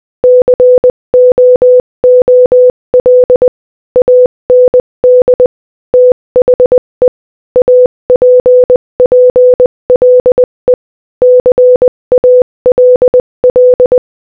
Morse code, heard as a keyed tone, about 20 wpm, 500 Hz. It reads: COOL AND THE APPLE CALL
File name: Morse-Code-Audio-Speed-20-2